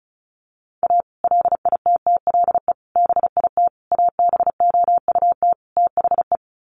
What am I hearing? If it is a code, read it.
Morse code, heard as a keyed tone, about 34 wpm, 700 Hz. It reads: A LITTLE BIT ABOUT THE
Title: Sets of 5 Words Encore2 35wpm